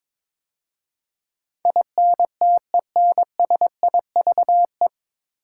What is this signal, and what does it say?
Morse code, heard as a keyed tone, about 22 wpm, 700 Hz. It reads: INTENSIVE
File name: 9 Letter Words - First 3 Letters Given - 22wpm